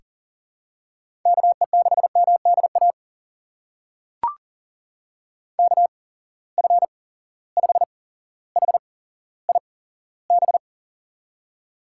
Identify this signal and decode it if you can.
Morse code, heard as a keyed tone, about 40 wpm, spaced as for 17 wpm, 700 Hz. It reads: KE6MDA KF5HIB
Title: USA Call Signs - 6 Characters - 8x Character Spacing - 40wpm